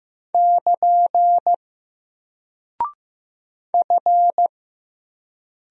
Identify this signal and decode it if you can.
Morse code, heard as a keyed tone, about 15 wpm, 700 Hz. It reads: <KN> F
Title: Single Character (Letter:Number:Punctuation:Prosign) 15wpm